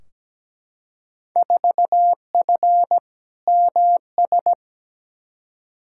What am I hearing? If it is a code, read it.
Morse code, heard as a keyed tone, about 17 wpm, 700 Hz. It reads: 4F MS
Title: Field Day Exchanges Encore - 17wpm